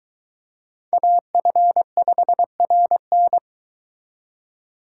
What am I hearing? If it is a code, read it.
Morse code, heard as a keyed tone, about 22 wpm, 700 Hz. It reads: AF5RN